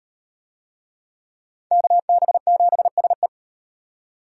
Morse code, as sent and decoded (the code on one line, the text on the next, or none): -.- -... --... ... .
KB7SE